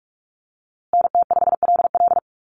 Morse code, heard as a keyed tone, about 44 wpm, 700 Hz. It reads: NT5LL
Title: USA Call Signs Repeated 3x 45wpm